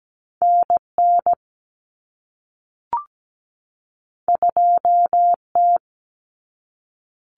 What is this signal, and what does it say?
Morse code, heard as a keyed tone, about 17 wpm, 700 Hz. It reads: NN 2T
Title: Common Cut Numbers - 2-Digit 17wpm